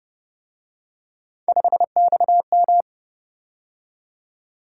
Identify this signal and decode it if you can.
Morse code, heard as a keyed tone, about 29 wpm, 700 Hz. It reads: 5XM